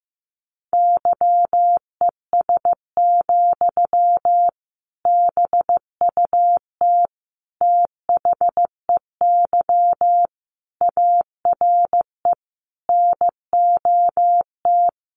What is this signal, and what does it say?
Morse code, heard as a keyed tone, about 15 wpm, 700 Hz. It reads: YES, BUT THEY ARE NOT
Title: Sentences from Top 1400 Words 15wpm